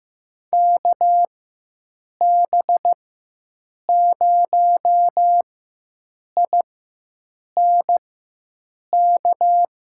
Morse code, heard as a keyed tone, about 15 wpm, spaced as for 7 wpm, 700 Hz. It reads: KB0INK
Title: USA Call Signs - 6 Characters - 4x Character Spacing - 15wpm